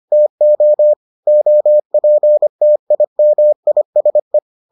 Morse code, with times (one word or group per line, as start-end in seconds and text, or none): 0.1-0.9 TO
1.3-4.4 OPTIMISE